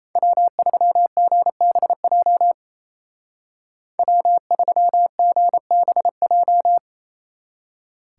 Morse code, heard as a keyed tone, about 34 wpm, 700 Hz. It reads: W3GBJ W3GBJ
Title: Speed Racing - USA 1x3 Call Signs 28wpm